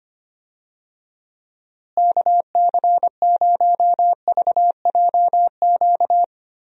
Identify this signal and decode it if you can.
Morse code, heard as a keyed tone, about 25 wpm, 700 Hz. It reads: KC0VJQ
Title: USA Call Signs - 6 Characters - Repeated 3x - 25wpm